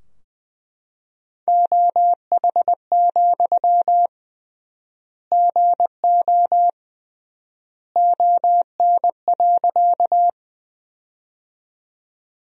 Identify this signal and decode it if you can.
Morse code, heard as a keyed tone, about 21 wpm, 700 Hz. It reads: OH, GO ON.
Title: 3x Word-Spacing - Sentences from Top 500 Words - 20wpm